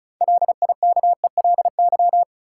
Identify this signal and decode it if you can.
Morse code, heard as a keyed tone, about 36 wpm, 700 Hz. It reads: LIKELY
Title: Sentences from Top 600 Words - New Word Drill - 35wpm